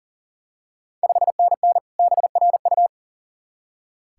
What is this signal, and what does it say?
Morse code, heard as a keyed tone, about 40 wpm, 690 Hz. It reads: 5NN BRU
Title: Ontario QSO Party Exchange 40wpm